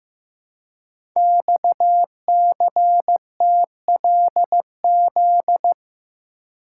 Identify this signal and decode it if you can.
Morse code, heard as a keyed tone, about 15 wpm, 700 Hz. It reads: XCTLZ
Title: Code Groups - 5 Letters - 15wpm